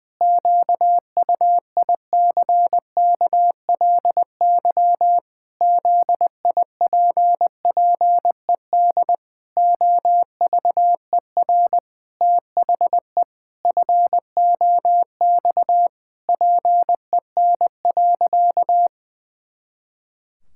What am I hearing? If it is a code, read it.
Morse code, heard as a keyed tone, about 20 wpm, 700 Hz. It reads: QUICKLY ZIPPED OVER THE FOX PEN.
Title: Pangram Challenge v2 - 20wpm